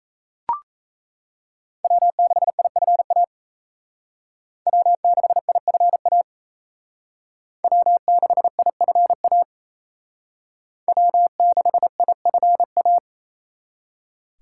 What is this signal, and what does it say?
Morse code, heard as a keyed tone, about 36 wpm, 700 Hz. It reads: W6IFA W6IFA W6IFA W6IFA